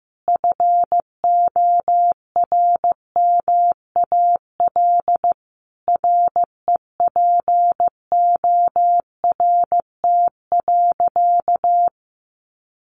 Morse code, heard as a keyed tone, about 15 wpm, 700 Hz. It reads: FORMAL REPORT.
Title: Sentences from Top 1300 Words 15wpm